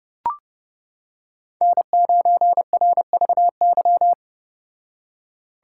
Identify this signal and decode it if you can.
Morse code, heard as a keyed tone, about 29 wpm, 700 Hz. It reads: N9RVY